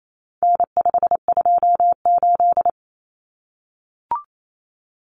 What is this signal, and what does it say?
Morse code, heard as a keyed tone, about 28 wpm, 700 Hz. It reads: N528
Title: Common Cut Numbers - 4-Digit 28wpm